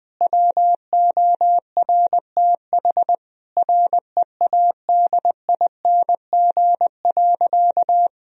Morse code, heard as a keyed tone, about 19 wpm, 700 Hz. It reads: WORTH READING.
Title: Sentences from Top 1000 Words - Review Part 3 - 20wpm